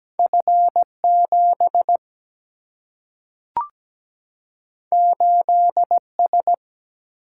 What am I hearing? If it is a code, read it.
Morse code, heard as a keyed tone, about 17 wpm, 700 Hz. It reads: F 7 8 S
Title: Single-Letter Single-Number 17wpm